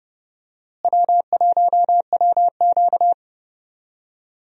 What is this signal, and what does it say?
Morse code, heard as a keyed tone, about 32 wpm, 700 Hz. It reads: W1WQ
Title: USA Call Signs - 4 Characters - Repeated 2x - 30wpm